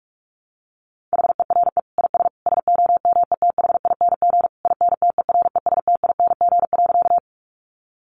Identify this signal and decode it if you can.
Morse code, heard as a keyed tone, about 44 wpm, 700 Hz. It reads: HERE IS SOMETHING INTERESTING.
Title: Sentences from Top 1000 Words 45wpm